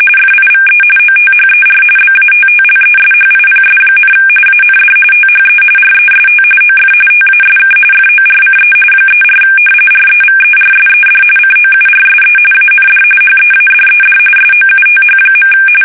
RTTY, about 45.45 baud. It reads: FIVE OTHER BATTLESHIPS, AND THE REST OF TASK FORCE 34 REVERSED COURSE AND HEADED SOUTH.
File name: fsk850 / battleshipnj